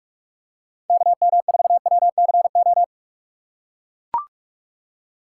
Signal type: Morse code

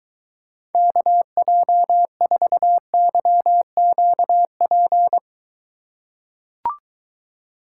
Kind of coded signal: Morse code

